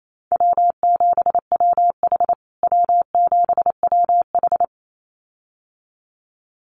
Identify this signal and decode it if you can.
Morse code, heard as a keyed tone, about 28 wpm, 700 Hz. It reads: W7WH W7WH
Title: USA Call Signs Repeated 2x 28wpm